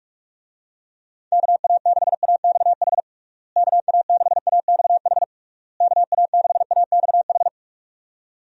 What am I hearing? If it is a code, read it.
Morse code, heard as a keyed tone, about 46 wpm, 700 Hz. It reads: KA6AXH KA6AXH KA6AXH
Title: USA Call Signs - 6 Characters - Repeated 3x - 45wpm